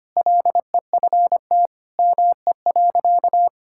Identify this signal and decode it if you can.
Morse code, heard as a keyed tone, about 25 wpm, 700 Hz. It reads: LEFT ME.
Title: Sentences from Top 1600 Words 25wpm